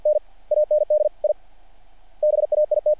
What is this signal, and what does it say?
Morse code, heard as a keyed tone, about 50 wpm, 600 Hz, with radio noise. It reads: N AND I BAIT
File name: archived / CRSnet